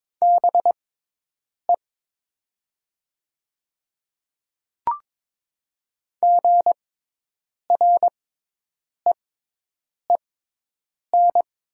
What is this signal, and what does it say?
Morse code, heard as a keyed tone, about 22 wpm, spaced as for 8 wpm, 700 Hz. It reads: BE GREEN
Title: 6x Character Spacing - Top 500 Words - 22wpm